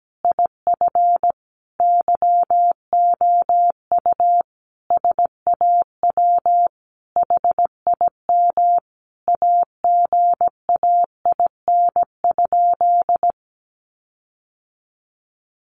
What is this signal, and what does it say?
Morse code, heard as a keyed tone, about 17 wpm, 700 Hz. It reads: IF YOU SAW HIM AGAIN?
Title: Sentences from Top 1400 Words 17wpm